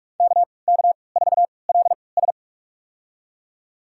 Morse code, heard as a keyed tone, about 46 wpm, spaced as for 25 wpm, 700 Hz. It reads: KK4LS
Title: USA Call Signs - 5 Characters - 3x Character Spacing - 45wpm